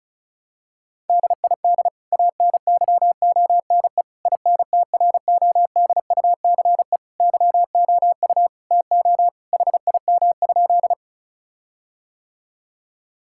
Morse code, heard as a keyed tone, about 36 wpm, 700 Hz. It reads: DID ANYONE INTRODUCE YOU TO HIM?
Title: Sentences from Top 1600 Words 35wpm